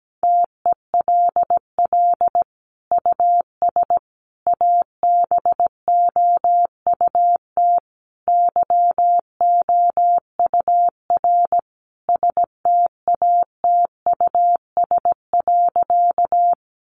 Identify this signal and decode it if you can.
Morse code, heard as a keyed tone, about 17 wpm, 700 Hz. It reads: TELL US ABOUT YOUR STATUS.
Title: Sentences from Top 1400 Words 17wpm